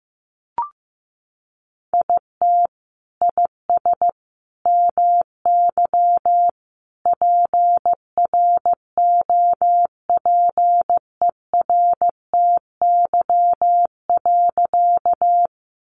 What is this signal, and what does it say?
Morse code, heard as a keyed tone, about 15 wpm, 700 Hz. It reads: IT IS MY PROPERTY.
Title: Sentences from Top 1000 Words 15wpm